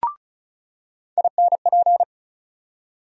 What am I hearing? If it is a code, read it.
Morse code, heard as a keyed tone, about 36 wpm, 700 Hz. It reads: INP
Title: USA Call Sign Postfixes - 3 Letters 35wpm